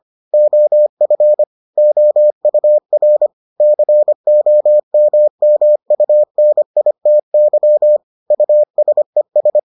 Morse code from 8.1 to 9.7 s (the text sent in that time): USES